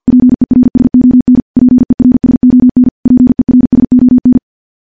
Morse code, heard as a keyed tone, about 50 wpm, 250 Hz. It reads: PERSON PERSON PERSON
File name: morse / morse-training-50WPM